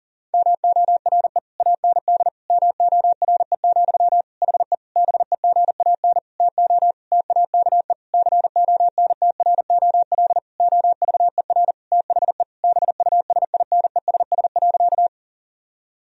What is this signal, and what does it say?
Morse code, heard as a keyed tone, about 37 wpm, 700 Hz. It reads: MORE AND MORE, HE BEGAN TO TAKE CONTROL OVER THE BUSINESS.
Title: Sentences from Top 1000 Words - Review Part 3 - 40wpm